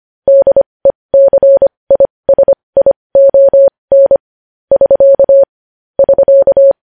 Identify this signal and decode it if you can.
Morse code, heard as a keyed tone, about 25 wpm, spaced as for 20 wpm, 550 Hz. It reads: DECISION <SK> <SK>
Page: News Headlines in Morse Code at 20 WPM